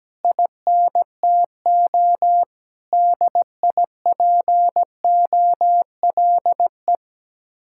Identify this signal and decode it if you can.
Morse code, heard as a keyed tone, about 17 wpm, 700 Hz. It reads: INTO DIPOLE
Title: Speed-Racing Common QSO Phrases Encore - 17wpm